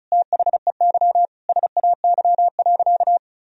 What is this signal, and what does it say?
Morse code, heard as a keyed tone, about 36 wpm, 700 Hz. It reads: THEY SAY.
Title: Sentences from Top 1000 Words - Review Part 2 - 35wpm